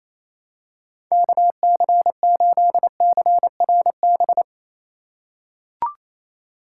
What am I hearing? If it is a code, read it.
Morse code, heard as a keyed tone, about 28 wpm, 700 Hz. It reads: KC8CRB
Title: USA Call Signs - 6 Characters - 4x Character Spacing - 28wpm